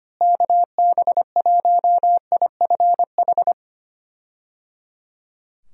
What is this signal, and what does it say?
Morse code, heard as a keyed tone, about 25 wpm, 700 Hz. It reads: KB1IFH
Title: USA Call Signs - 6 Characters - Repeated 3x - 25wpm